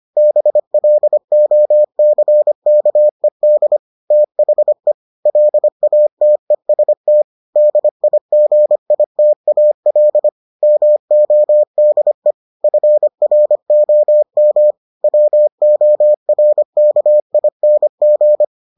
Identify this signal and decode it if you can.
Morse code, heard as a keyed tone, about 25 wpm, 600 Hz. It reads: BLOCKED THE LATEST DIGITAL MODE FROM WORKING